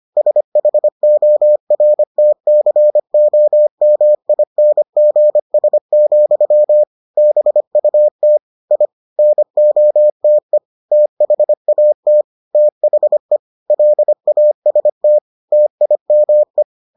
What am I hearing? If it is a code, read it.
Morse code, heard as a keyed tone, about 25 wpm, 600 Hz. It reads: SHORTCOMINGS, BUT I NOTE THAT THE LAST TIME